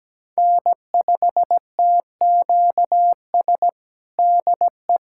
Morse code, heard as a keyed tone, about 17 wpm, 700 Hz. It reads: N5TQS DE
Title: USA Callsign DE Callsign - 5 Characters - 17wpm